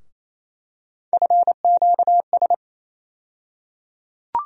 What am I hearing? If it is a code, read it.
Morse code, heard as a keyed tone, about 27 wpm, 700 Hz. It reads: FQS
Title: USA Call Sign Postfixes - 3 Letters 28wpm